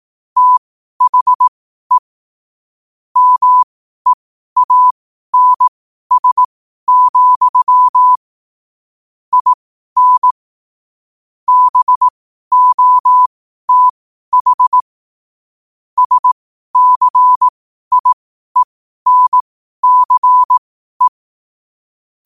Morse code, heard as a keyed tone, about 18 wpm, spaced as for 12 wpm, 1000 Hz. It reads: THE MEANS, IN BOTH SCIENCE
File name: qotdfiles / 12wpm